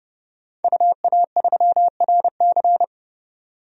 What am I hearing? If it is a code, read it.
Morse code, heard as a keyed tone, about 30 wpm, 700 Hz. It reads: UA3RC
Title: DXCC-100 Call Signs - 5 Characters - 30wpm